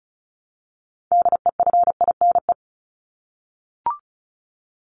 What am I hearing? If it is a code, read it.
Morse code, heard as a keyed tone, about 34 wpm, 700 Hz. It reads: DEFINE
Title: ED Suffix - Top 100 Words with Base Words 35wpm